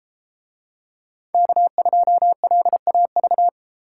Morse code, heard as a keyed tone, about 32 wpm, 700 Hz. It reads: K2LAV